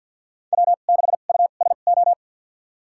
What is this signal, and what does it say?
Morse code, heard as a keyed tone, about 53 wpm, spaced as for 37 wpm, 690 Hz. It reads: W6USO
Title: USA Call Signs - 2x Character Spacing - 50wpm